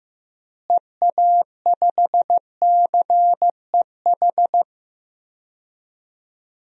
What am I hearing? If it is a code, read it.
Morse code, heard as a keyed tone, about 15 wpm, 700 Hz. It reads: EA5CEH
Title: DXCC-100 Call Signs - 6 Characters - Repeated 2x - 15wpm